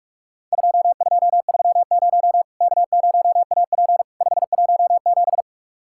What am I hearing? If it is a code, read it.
Morse code, heard as a keyed tone, about 46 wpm, 700 Hz. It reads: 2230 K0AP 517
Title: Worked All Europe DX Contest - QTC Exchange - 45wpm